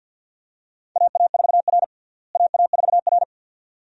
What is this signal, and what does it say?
Morse code, heard as a keyed tone, about 50 wpm, 700 Hz. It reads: AA4R AA4R
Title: USA Call Signs - 4 Characters - Repeated 2x - 50wpm